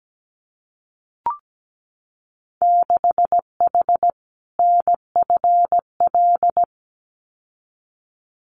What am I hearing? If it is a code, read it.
Morse code, heard as a keyed tone, about 17 wpm, 700 Hz. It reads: 6H NFL
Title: Winter Field Day Exchanges Encore 17wpm